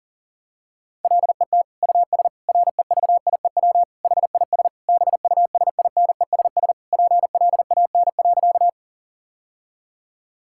Morse code, heard as a keyed tone, about 40 wpm, 700 Hz. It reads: LET US REVIEW HIS BUSINESS PLAN.
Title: Sentences from Top 1000 Words - Review Part 6 - 40wpm